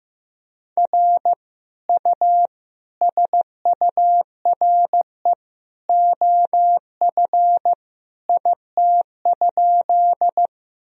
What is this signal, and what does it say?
Morse code, heard as a keyed tone, about 16 wpm, 700 Hz. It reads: R U SURE OF IT?
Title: Sentences from Top 300 Words and Abbreviations 15wpm